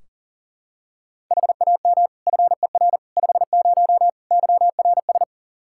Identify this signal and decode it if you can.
Morse code, heard as a keyed tone, about 40 wpm, 700 Hz. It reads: HAM FER 50 YRS
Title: Practice QSO - Exchange 3 - More Info - 40wpm